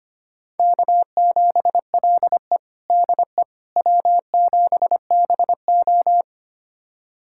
Morse code, heard as a keyed tone, about 25 wpm, 700 Hz. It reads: K7LE DE W7BO
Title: USA Callsign DE Callsign - 4 Characters - 25wpm